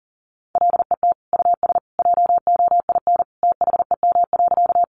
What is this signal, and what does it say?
Morse code, heard as a keyed tone, about 40 wpm, 700 Hz. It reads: LET US JOIN THEM.
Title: Sentences from Top 1400 Words 40wpm